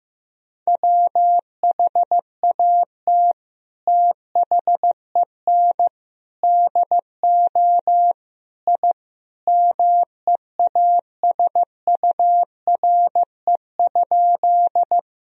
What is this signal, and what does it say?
Morse code, heard as a keyed tone, about 15 wpm, 700 Hz. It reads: WHAT THEN DO I MEASURE?
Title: Sentences from Top 1000 Words - Review Part 2 - 15wpm